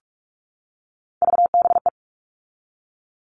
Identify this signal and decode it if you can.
Morse code, heard as a keyed tone, about 44 wpm, 700 Hz. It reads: VBE
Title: All Cut Numbers - 3-Digit 45wpm